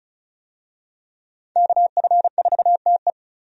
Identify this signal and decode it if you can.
Morse code, heard as a keyed tone, about 36 wpm, 700 Hz. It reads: KF4TE
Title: USA Call Signs - 5 Characters - 35wpm